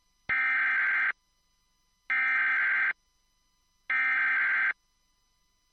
Emergency Alert System (SAME) tones (header bursts), heard as harsh buzzing bursts. It is heard from 0.3 to 4.7 s.
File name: wrct-log hi- 20